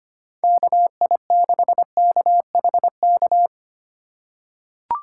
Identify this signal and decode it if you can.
Morse code, heard as a keyed tone, about 25 wpm, 700 Hz. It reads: KI6KHK
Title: USA Call Signs - 3x Character Spacing - 25wpm